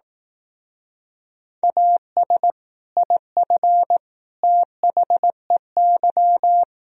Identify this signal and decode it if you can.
Morse code, heard as a keyed tone, about 18 wpm, 700 Hz. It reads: AS IF THEY
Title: Speed-Racing - Sets of 4 Words 15wpm